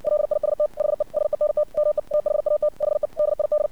Morse code, heard as a keyed tone, about 60 wpm, 600 Hz. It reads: LIST HE SENT ME THAT HE DID
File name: archived / CRSnet